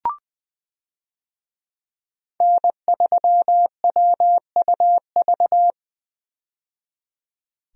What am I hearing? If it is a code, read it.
Morse code, heard as a keyed tone, about 20 wpm, 700 Hz. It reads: N3WUV
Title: USA Call Signs - 5 Characters - 20wpm